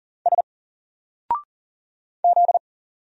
Morse code, heard as a keyed tone, about 39 wpm, 700 Hz. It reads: S Z
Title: Single Letter-Number 40wpm